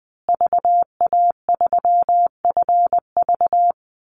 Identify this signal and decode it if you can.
Morse code, heard as a keyed tone, about 20 wpm, 700 Hz. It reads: VA3FV
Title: Call Signs Repeated 2x - 20wpm